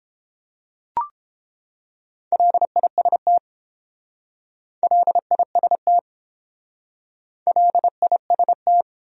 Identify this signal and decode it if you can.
Morse code, heard as a keyed tone, about 30 wpm, 700 Hz. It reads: LIST LIST LIST